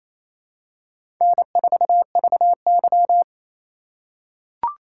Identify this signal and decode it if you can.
Morse code, heard as a keyed tone, about 27 wpm, 690 Hz. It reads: N4VY
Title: USA Call Signs - 4 Characters - 2x Character Spacing - 28wpm